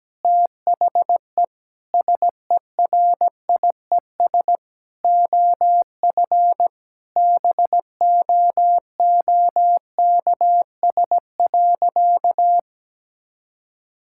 Morse code, heard as a keyed tone, about 17 wpm, 700 Hz. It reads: THE SERIES OF BOOKS.
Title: Sentences from Top 1000 Words - Review Part 5 - 17wpm